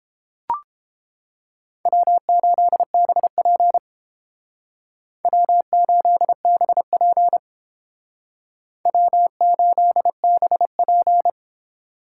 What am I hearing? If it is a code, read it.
Morse code, heard as a keyed tone, about 30 wpm, 700 Hz. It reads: W8BP W8BP W8BP